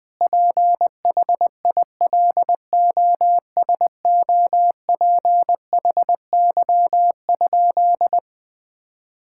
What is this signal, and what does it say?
Morse code, heard as a keyed tone, about 19 wpm, 700 Hz. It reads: PHILOSOPHY?
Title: Sentences from Top 1000 Words - Review Part 6 - 20wpm